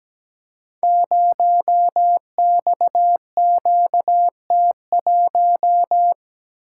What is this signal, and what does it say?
Morse code, heard as a keyed tone, about 17 wpm, 700 Hz. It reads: 0XQT1
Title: Code Groups - 5 Letter:Numbers - 17wpm